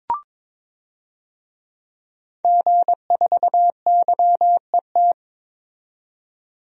Morse code, heard as a keyed tone, about 22 wpm, 700 Hz. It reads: G4YET
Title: DXCC-100 Call Signs - 5 Characters - 22wpm